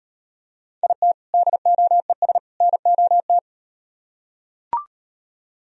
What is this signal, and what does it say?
Morse code, heard as a keyed tone, about 37 wpm, 700 Hz. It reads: IT DOES NOT